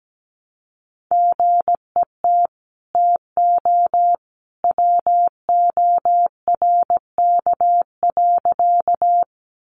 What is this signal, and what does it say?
Morse code, heard as a keyed tone, about 17 wpm, 700 Hz. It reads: GET TO WORK.
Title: Sentences from Top 200 Words 17wpm